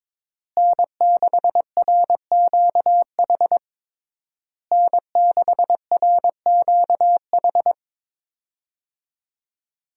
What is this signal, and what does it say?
Morse code, heard as a keyed tone, about 22 wpm, 700 Hz. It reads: N6RQH N6RQH
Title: USA Call Signs - 5 Characters - Repeated 3x - 22wpm